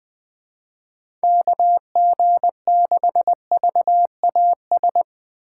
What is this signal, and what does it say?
Morse code, heard as a keyed tone, about 20 wpm, 700 Hz. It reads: KG6VAS